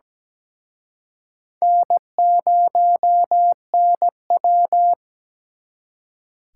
Morse code, heard as a keyed tone, about 20 wpm, 700 Hz. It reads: N0NW
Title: Speed Racing - USA 1x2 Call Signs 17wpm